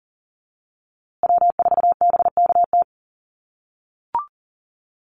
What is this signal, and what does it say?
Morse code, heard as a keyed tone, about 40 wpm, 700 Hz. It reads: W4BKT
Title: USA Call Signs Repeated 2x 40wpm